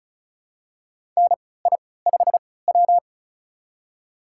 Morse code, heard as a keyed tone, about 36 wpm, spaced as for 20 wpm, 700 Hz. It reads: NI5W
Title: USA Call Signs - 4 Characters - 3x Character Spacing - 35wpm